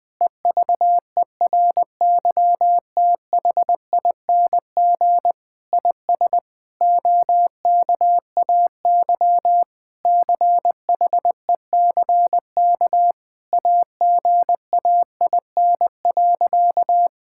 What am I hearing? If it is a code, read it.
Morse code, heard as a keyed tone, about 21 wpm, 700 Hz. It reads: EVERYTHING IS OKAY CHECK AGAIN.
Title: Sentences with 4-Word Phrases - 20wpm